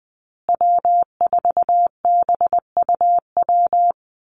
Morse code, heard as a keyed tone, about 20 wpm, 700 Hz. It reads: W4BUW
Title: Call Signs Repeated 2x - 20wpm